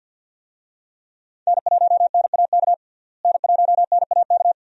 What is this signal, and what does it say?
Morse code, heard as a keyed tone, about 53 wpm, 700 Hz. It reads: N1NAK N1NAK
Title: USA Call Signs - 5 Characters - Repeated 2x - 50wpm